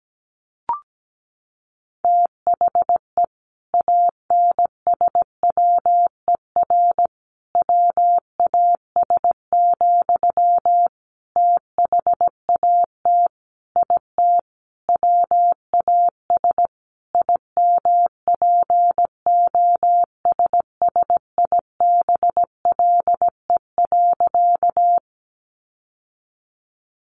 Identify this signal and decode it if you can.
Morse code, heard as a keyed tone, about 17 wpm, 700 Hz. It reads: THE ANSWER WAS, THAT IT WAS IMPOSSIBLE.
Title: Sentences from Top 1100 Words 17wpm